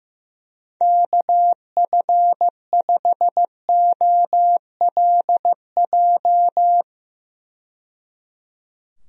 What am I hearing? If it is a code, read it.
Morse code, heard as a keyed tone, about 15 wpm, 700 Hz. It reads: KF5OLJ
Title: USA Call Signs - 6 Characters - 15wpm